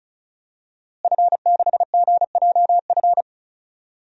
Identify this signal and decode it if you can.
Morse code, heard as a keyed tone, about 34 wpm, 700 Hz. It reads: F6GJF
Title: DXCC-100 Call Signs - 5 Characters - Repeated 3x - 35wpm